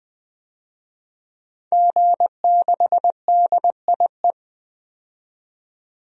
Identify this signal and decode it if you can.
Morse code, heard as a keyed tone, about 20 wpm, 700 Hz. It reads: G6DIE
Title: DXCC-100 Call Signs - 5 Characters - 20wpm